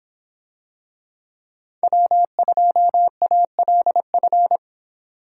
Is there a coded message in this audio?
Yes: Morse code from 1.8 to 4.6 s, heard as a keyed tone, about 26 wpm, 700 Hz. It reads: W2ALF